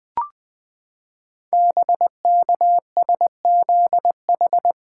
Morse code, heard as a keyed tone, about 20 wpm, 700 Hz. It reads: BKSZH
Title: Code Groups - 5 Letters - 20wpm